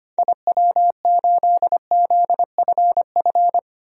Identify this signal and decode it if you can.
Morse code, heard as a keyed tone, about 25 wpm, 700 Hz. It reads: IW8ZFF
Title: DXCC-100 Call Signs - 6 Characters - 25wpm